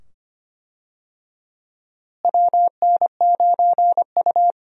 Morse code, heard as a keyed tone, about 25 wpm, 700 Hz. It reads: WN9U
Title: USA Call Signs - 4 Characters - Repeated 2x - 25wpm